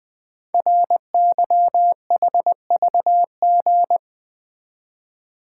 Morse code, heard as a keyed tone, about 20 wpm, 700 Hz. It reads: RYHVG
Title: Code Groups - 5 Letters - 20wpm